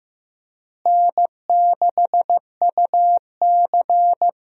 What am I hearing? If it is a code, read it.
Morse code, heard as a keyed tone, about 15 wpm, 700 Hz. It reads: N6UC
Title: USA Call Signs - 4 Characters - Repeated 3x - 15wpm